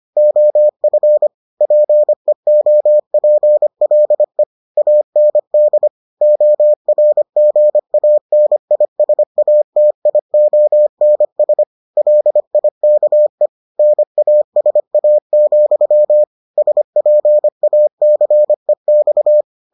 Morse code, heard as a keyed tone, about 25 wpm, 600 Hz. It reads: OF PEOPLE AND ORGANISATIONS LIKE NASA, SPACEX